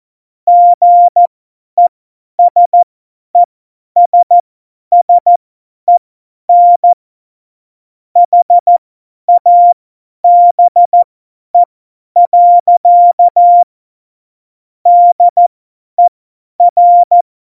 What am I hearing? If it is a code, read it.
Morse code, heard as a keyed tone, about 14 wpm, spaced as for 10 wpm, 700 Hz. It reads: GESESSEN HABE. DER
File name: Morsekurs / Witze / Witz 12 Mai 2020, 50 F70